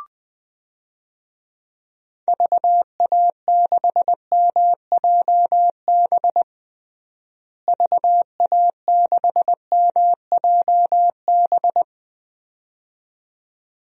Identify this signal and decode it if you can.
Morse code, heard as a keyed tone, about 21 wpm, 700 Hz. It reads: VA6MJB VA6MJB
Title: DXCC-100 Call Signs - 6 Characters - Repeated 2x - 20wpm